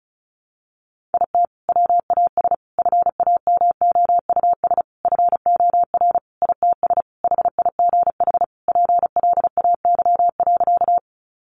Morse code, heard as a keyed tone, about 34 wpm, 700 Hz. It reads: IT WAS FAMOUS FOR ITS HIGH PLAY.
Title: Sentences from Top 1500 Words 35wpm